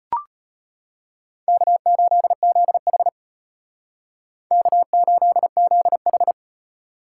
Morse code, heard as a keyed tone, about 32 wpm, 700 Hz. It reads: K8ZH K8ZH